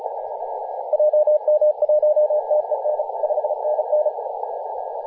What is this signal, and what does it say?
Morse code, heard as a keyed tone, about 32 wpm, 600 Hz, with radio noise. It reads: JM1TEN 5NN
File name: posts / 2016-nov-14-0725z-tl8ao-14-023mhz-cw